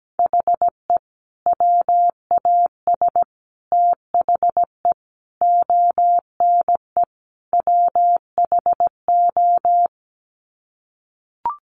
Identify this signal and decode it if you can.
Morse code, heard as a keyed tone, about 17 wpm, 700 Hz. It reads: HE WAS THE ONE WHO
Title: Sets of 5 Words Encore2 17wpm